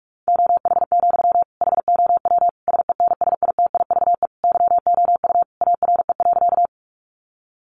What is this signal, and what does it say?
Morse code, heard as a keyed tone, about 44 wpm, 700 Hz. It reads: OH, HOW SENSITIVE YOU ARE.
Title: Sentences from Top 1400 Words 45wpm